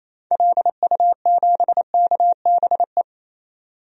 Morse code, heard as a keyed tone, about 28 wpm, 700 Hz. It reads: LU7KBE
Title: DXCC-100 Call Signs - 6 Characters - 28wpm